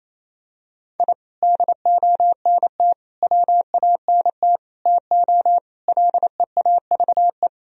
Morse code, heard as a keyed tone, about 28 wpm, 700 Hz. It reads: I DONT WANT TO LEAVE
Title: Sentences from Top 1000 Words - Review Part 1 - 28wpm